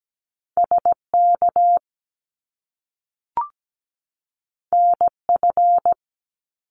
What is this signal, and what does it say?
Morse code, heard as a keyed tone, about 17 wpm, 700 Hz. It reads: SK NF
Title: Top Two Letter Combinations in Words 17wpm